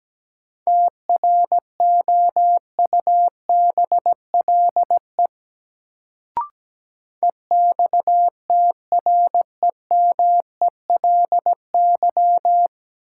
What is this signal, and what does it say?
Morse code, heard as a keyed tone, about 17 wpm, 700 Hz. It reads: TROUBLE EXTREMELY
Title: Sentences from Top 1200 Words - New Word Drill Alt - 17wpm